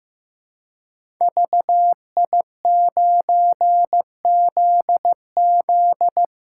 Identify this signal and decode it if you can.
Morse code, heard as a keyed tone, about 15 wpm, 700 Hz. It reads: VI9ZZ
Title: Code Groups - 5 Letter:Numbers - 15wpm